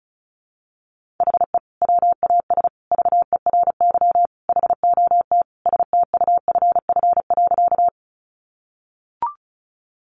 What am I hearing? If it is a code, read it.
Morse code, heard as a keyed tone, about 34 wpm, 700 Hz. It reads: HE WAS VERY HOT STUFF.
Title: Sentences from Top 1400 Words 35wpm